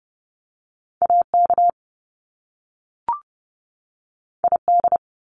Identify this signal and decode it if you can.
Morse code, heard as a keyed tone, about 30 wpm, 700 Hz. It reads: AK ID
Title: US States 30wpm